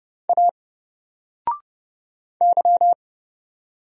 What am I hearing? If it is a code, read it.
Morse code, heard as a keyed tone, about 30 wpm, 700 Hz. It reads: A Y
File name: Single-Letter-Number-30wpm